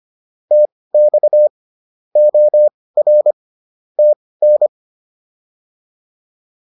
Morse code, heard as a keyed tone, about 25 wpm, spaced as for 18 wpm, 600 Hz. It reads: TX OR TN